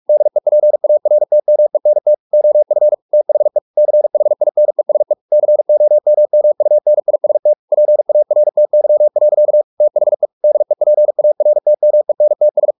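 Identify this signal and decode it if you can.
Morse code, heard as a keyed tone, about 44 wpm, 600 Hz. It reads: DEPARTMENT OF THE CHINESE COMMUNIST PARTY. THE DEPARTMENTS
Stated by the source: Morse practice at 45 wpm, the novel Never by Ken Follett in English, from CW med Gnister.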